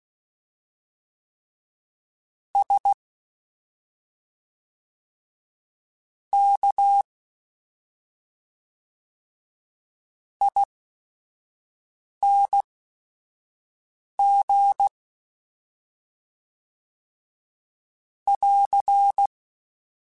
Morse code, heard as a keyed tone, about 16 wpm, spaced as for 5 wpm, 800 Hz. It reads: S K ING <AR>
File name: Special INGs / Special INGs-mp3s / SlowToFaster ing @5